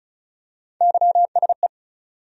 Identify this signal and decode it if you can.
Morse code, heard as a keyed tone, about 36 wpm, 700 Hz. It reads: YSE
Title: USA Call Sign Postfixes - 3 Letters 35wpm